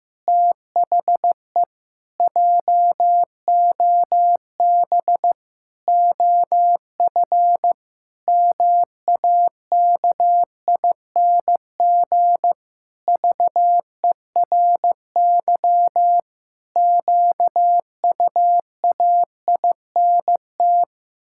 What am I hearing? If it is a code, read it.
Morse code, heard as a keyed tone, about 15 wpm, 700 Hz. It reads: THE JOB OF MAKING VERY QUAINT
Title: Pangram Challenge v2 - 15wpm